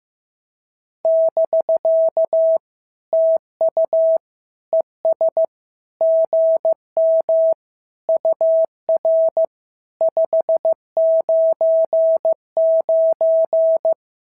Morse code, heard as a keyed tone, about 15 wpm, 650 Hz. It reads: <BK> TU ES GM UR 599
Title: SOTA QSO Practice 15wpm